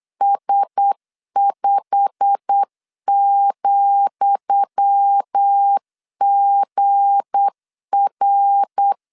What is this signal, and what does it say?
Morse code, heard as a keyed tone, about 8 wpm, 800 Hz. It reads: S5,GR
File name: LEZIONI / 10 SIGNS / 10D COMA 8wpm 3749